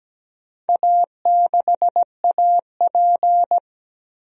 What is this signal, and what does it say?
Morse code, heard as a keyed tone, about 17 wpm, 700 Hz. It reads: A6AP
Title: DXCC-100 Call Signs - 4 Characters - 17wpm